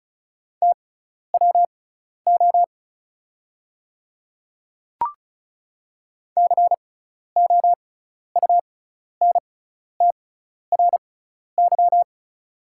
Morse code, heard as a keyed tone, about 36 wpm, spaced as for 12 wpm, 700 Hz. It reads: TWO COUNTRY
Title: 6x Character Spacing - Top 300 Words - 35wpm